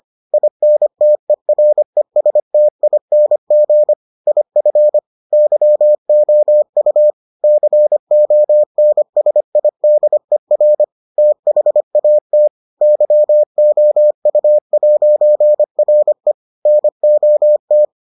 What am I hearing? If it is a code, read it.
Morse code, heard as a keyed tone, about 25 wpm, 600 Hz. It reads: INTERESTING IF YOU CONSIDER THAT YOU'RE NOT